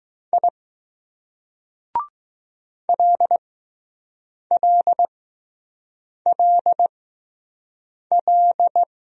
Morse code, heard as a keyed tone, about 20 wpm, spaced as for 6 wpm, 700 Hz. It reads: I LLLL